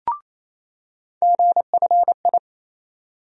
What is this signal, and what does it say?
Morse code, heard as a keyed tone, about 28 wpm, 700 Hz. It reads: GFI
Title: USA Call Sign Postfixes - 3 Letters 28wpm